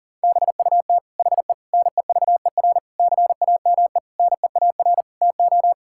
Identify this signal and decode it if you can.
Morse code, heard as a keyed tone, about 40 wpm, 700 Hz. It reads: BUT HE NEVER CAME NEAR TO
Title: Sentences from Top 1600 Words 40wpm